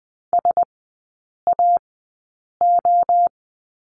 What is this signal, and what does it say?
Morse code, heard as a keyed tone, about 20 wpm, spaced as for 8 wpm, 700 Hz. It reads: SAO
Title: 133 Lesson 22e - Question Mark - Three Characters 20wpm